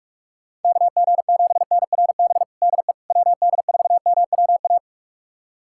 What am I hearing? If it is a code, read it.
Morse code, heard as a keyed tone, about 46 wpm, 690 Hz. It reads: KG7NRB DE WD4MWA
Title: USA Callsign DE Callsign 45wpm